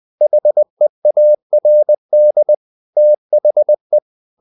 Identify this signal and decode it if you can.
Morse code, heard as a keyed tone, about 20 wpm, 600 Hz. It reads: HEARD THE
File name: the old man and the sea / the old man and the sea part 1 20wpm